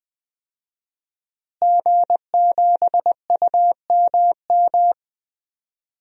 Morse code, heard as a keyed tone, about 20 wpm, 700 Hz. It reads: G7UMM
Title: DXCC-100 Call Signs - 5 Characters - Repeated 2x - 20wpm